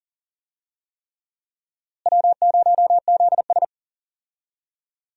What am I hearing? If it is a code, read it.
Morse code, heard as a keyed tone, about 40 wpm, 700 Hz. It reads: W0ZS
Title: USA Call Signs - 4 Characters - Repeated 3x - 40wpm